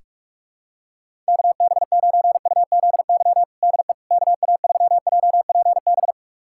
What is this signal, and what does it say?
Morse code, heard as a keyed tone, about 46 wpm, 700 Hz. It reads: KB9UZY DE KA3JPB
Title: USA Callsign DE Callsign - 6 Characters - 45wpm